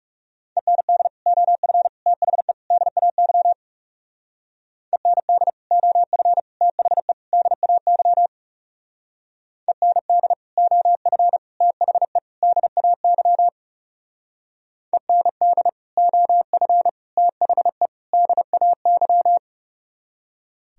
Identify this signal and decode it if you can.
Morse code, heard as a keyed tone, about 37 wpm, 700 Hz. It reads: END OF THE DAY END OF THE DAY END OF THE DAY END OF THE DAY
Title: Speed-Racing - Sets of 4 Words 30wpm